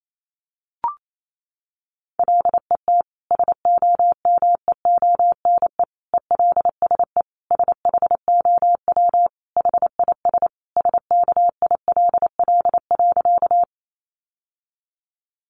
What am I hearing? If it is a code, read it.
Morse code, heard as a keyed tone, about 28 wpm, 700 Hz. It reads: LET SOMEONE ELSE SHOW HIS SKILL.
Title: Sentences from Top 1200 Words 28wpm